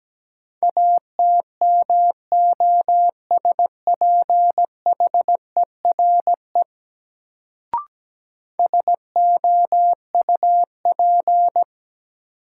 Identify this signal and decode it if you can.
Morse code, heard as a keyed tone, about 17 wpm, 700 Hz. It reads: ATMOSPHERE SOUP
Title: Sentences from Top 1600 Words - New Word Drill Alt - 17wpm